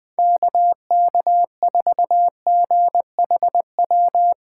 Morse code, heard as a keyed tone, about 20 wpm, 700 Hz. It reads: KK4GHW
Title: USA Call Signs - 6x Character Spacing - 20wpm